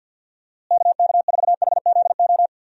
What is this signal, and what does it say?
Morse code, heard as a keyed tone, about 51 wpm, 700 Hz. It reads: KK4HZO
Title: USA Call Signs - 6 Characters - Repeated 2x - 50wpm